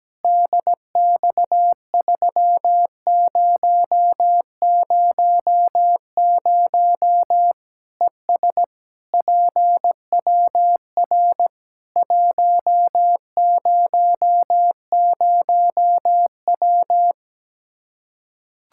Morse code, heard as a keyed tone, about 17 wpm, 700 Hz. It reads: DX3000 ES PWR 100W
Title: Practice QSO - Exchange 2 - Info - 17wpm